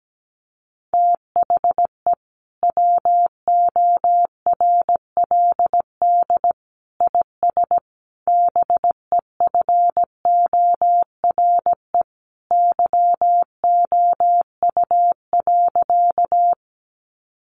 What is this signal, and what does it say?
Morse code, heard as a keyed tone, about 17 wpm, 700 Hz. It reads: THE WORLD IS BEFORE YOU.
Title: Sentences from Top 200 Words 17wpm